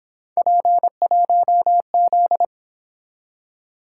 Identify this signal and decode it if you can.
Morse code, heard as a keyed tone, about 27 wpm, 700 Hz. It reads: P1Z
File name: Speed-Racing - Three Letter-Number Combinations - 22wpm